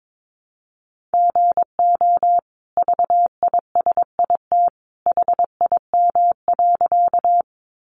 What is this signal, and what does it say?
Morse code, heard as a keyed tone, about 22 wpm, 700 Hz. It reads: GO VISIT HIM.
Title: Sentences from Top 900 Words 22wpm